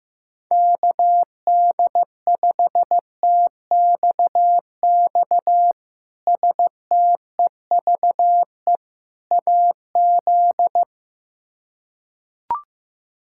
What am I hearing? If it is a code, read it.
Morse code, heard as a keyed tone, about 15 wpm, 700 Hz. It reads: KD5TXX STEVE AZ
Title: K1USN SST QSO Practice - 15wpm